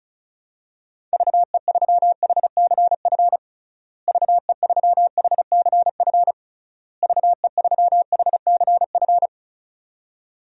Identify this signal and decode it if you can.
Morse code, heard as a keyed tone, about 36 wpm, 700 Hz. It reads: VE3HCF VE3HCF VE3HCF
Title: DXCC-100 Call Signs - 6 Characters - Repeated 3x - 35wpm